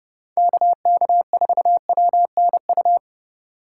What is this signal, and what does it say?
Morse code, heard as a keyed tone, about 40 wpm, 700 Hz. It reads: KK4WNU